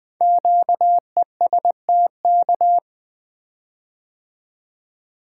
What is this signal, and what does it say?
Morse code, heard as a keyed tone, about 20 wpm, 700 Hz. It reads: QESTK
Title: Code Groups - 5 Letters - 20wpm